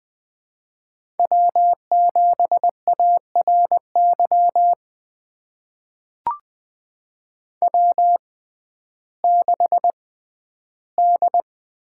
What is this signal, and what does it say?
Morse code, heard as a keyed tone, about 20 wpm, spaced as for 12 wpm, 700 Hz. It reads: W7ARY W6D
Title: USA Call Signs - 5 Characters - 6x Character Spacing - 20wpm